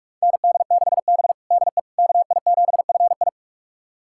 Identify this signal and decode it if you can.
Morse code, heard as a keyed tone, about 46 wpm, 700 Hz. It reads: ND6B DE KI7FI
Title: USA Callsign DE Callsign 45wpm